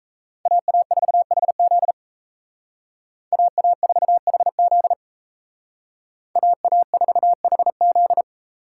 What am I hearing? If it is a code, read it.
Morse code, heard as a keyed tone, about 35 wpm, 700 Hz. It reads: AA4HZ AA4HZ AA4HZ
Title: Speed Racing - USA 2x2 Call Signs 28wpm